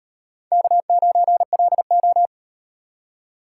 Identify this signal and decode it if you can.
Morse code, heard as a keyed tone, about 38 wpm, 700 Hz. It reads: K9LO